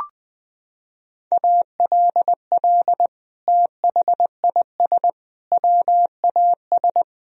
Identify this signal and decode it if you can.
Morse code, heard as a keyed tone, about 20 wpm, 700 Hz. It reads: ALL THIS WAS
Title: Sentences from Top 1000 Words - Review Part 2 - 20wpm